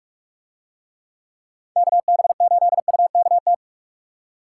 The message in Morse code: KB8UKT

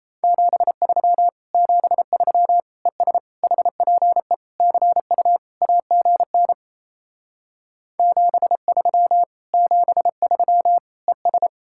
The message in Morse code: 73 73 ES HPE CU AGN 73 73 ES

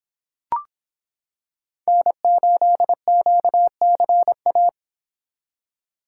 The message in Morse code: N8QCA

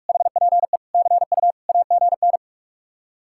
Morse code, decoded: HPE CU AGN

